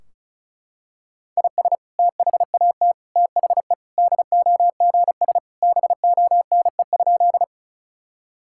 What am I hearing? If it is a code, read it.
Morse code, heard as a keyed tone, about 36 wpm, 700 Hz. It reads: IS THAT THE DOGS BONE?
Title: Sentences from Top 1600 Words 35wpm